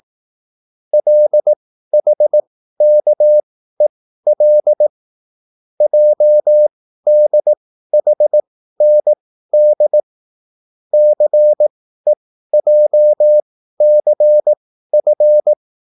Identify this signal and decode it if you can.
Morse code, heard as a keyed tone, about 18 wpm, spaced as for 13 wpm, 600 Hz. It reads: LHKEL JDHND CEJCF